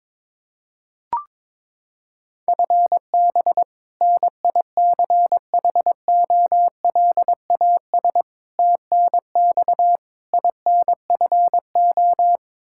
Morse code, heard as a keyed tone, about 22 wpm, 700 Hz. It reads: FB NICHOLAS TNX INFO
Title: K1USN SST QSO Practice 22wpm